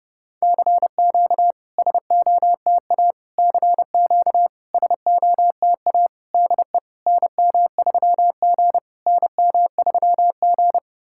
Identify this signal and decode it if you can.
Morse code, heard as a keyed tone, about 30 wpm, 700 Hz. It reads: CQ SOTA CQ SOTA DE NM3G NM3G
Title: SOTA QSO Practice 30wpm